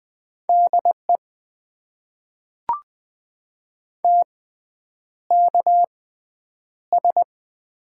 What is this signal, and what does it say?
Morse code, heard as a keyed tone, about 20 wpm, spaced as for 9 wpm, 700 Hz. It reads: DE TKS
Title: Top 90 QSO Elements - 6x Character Spacing - 20wpm